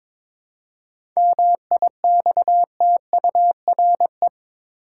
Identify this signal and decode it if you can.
Morse code, heard as a keyed tone, about 22 wpm, 700 Hz. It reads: MIXTURE
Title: Seven Letter Words - 22wpm